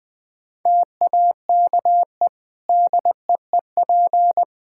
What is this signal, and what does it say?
Morse code, heard as a keyed tone, about 21 wpm, 700 Hz. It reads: TAKE DEEP
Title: Easy Sentences with Hints - 20wpm